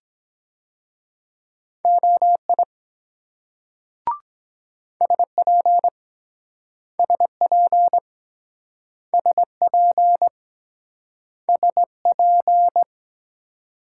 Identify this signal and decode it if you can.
Morse code, heard as a keyed tone, about 23 wpm, 700 Hz. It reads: OI SP SP SP SP